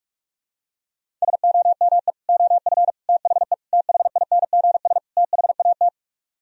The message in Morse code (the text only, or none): SOME OF THE THINGS THAT